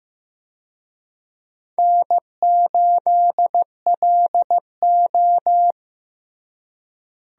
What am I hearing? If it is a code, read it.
Morse code, heard as a keyed tone, about 15 wpm, 700 Hz. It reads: N8LO
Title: USA Call Signs - 4 Characters - Repeated 2x - 15wpm